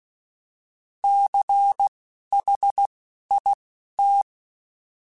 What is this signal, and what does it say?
Morse code, heard as a keyed tone, about 16 wpm, spaced as for 11 wpm, 800 Hz. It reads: CHIT
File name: Alphabetical / Alphabetical mp3s / Cs @10